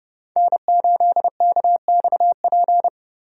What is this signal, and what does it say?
Morse code, heard as a keyed tone, about 30 wpm, 700 Hz. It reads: N8KXP
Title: USA Call Signs - 5 Characters - Repeated 3x - 30wpm